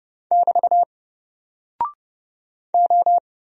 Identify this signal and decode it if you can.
Morse code, heard as a keyed tone, about 31 wpm, 700 Hz. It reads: <BT> O
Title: Single Character (Letter:Number:Punctuation:Prosign) 30wpm